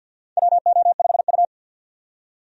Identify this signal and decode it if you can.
Morse code, heard as a keyed tone, about 53 wpm, 700 Hz. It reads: WO5U
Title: USA Call Signs - 4 Characters - 50wpm